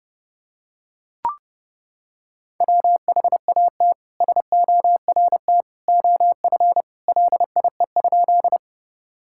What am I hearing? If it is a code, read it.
Morse code, heard as a keyed tone, about 30 wpm, 700 Hz. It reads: WHAT SORT OF LIE?
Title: Sentences from Top 1600 Words 30wpm